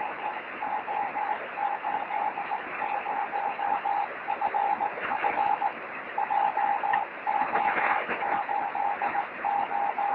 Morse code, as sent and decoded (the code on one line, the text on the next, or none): none